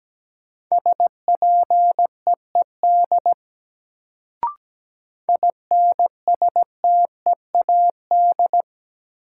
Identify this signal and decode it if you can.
Morse code, heard as a keyed tone, about 17 wpm, 700 Hz. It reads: SPEED INSTEAD
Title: Sentences from Top 800 Words - New Word Drill - 17wpm